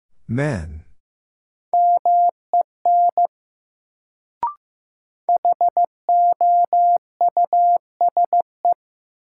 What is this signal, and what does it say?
Morse code, heard as a keyed tone, about 15 wpm, 700 Hz. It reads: MEN HOUSE
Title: Sentences from Top 200 Words - New Word Drill - 15wpm